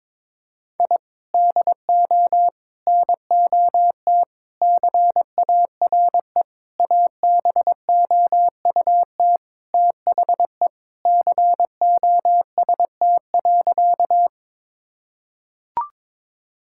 Morse code, heard as a keyed tone, about 21 wpm, 700 Hz. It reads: I DO NOT CARE ABOUT THE COST.
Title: Sentences from Top 1000 Words - Review Part 5 - 22wpm